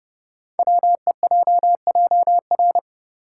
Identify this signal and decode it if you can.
Morse code, heard as a keyed tone, about 30 wpm, 700 Hz. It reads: WEJJR
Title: Code Groups - 5 Letters - 30wpm